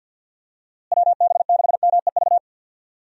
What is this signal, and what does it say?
Morse code, heard as a keyed tone, about 51 wpm, 700 Hz. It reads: WB6MEV
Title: USA Call Signs - 6 Characters - Repeated 3x - 50wpm